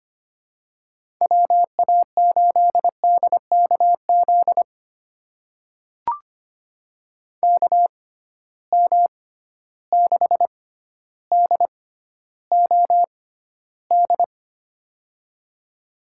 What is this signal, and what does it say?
Morse code, heard as a keyed tone, about 25 wpm, spaced as for 13 wpm, 700 Hz. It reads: WA8DKZ KM6DOD
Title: USA Call Signs - 6 Characters - 6x Character Spacing - 25wpm